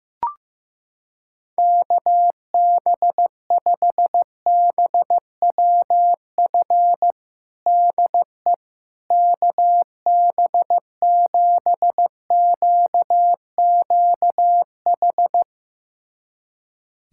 Morse code, heard as a keyed tone, about 15 wpm, 700 Hz. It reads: KB5BWF DE KB7QQH
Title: USA Callsign DE Callsign 15wpm